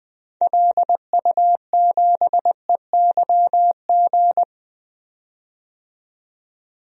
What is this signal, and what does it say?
Morse code, heard as a keyed tone, about 20 wpm, 700 Hz. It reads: LU7EYG
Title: DXCC-100 Call Signs - 6 Characters - 20wpm